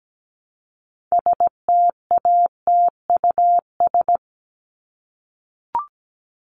Morse code, heard as a keyed tone, about 17 wpm, 700 Hz. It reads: STATUS
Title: Top 1251-1500 Words 17wpm